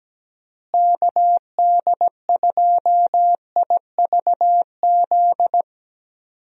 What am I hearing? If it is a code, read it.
Morse code, heard as a keyed tone, about 17 wpm, 700 Hz. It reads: KD2IVZ
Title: USA Call Signs - 6 Characters - Repeated 2x - 17wpm